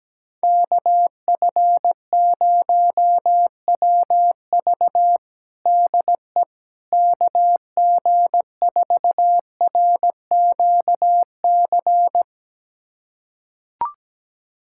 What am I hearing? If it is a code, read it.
Morse code, heard as a keyed tone, about 17 wpm, 700 Hz. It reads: KF0WV DE KG4RQC
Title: USA Callsign DE Callsign 17wpm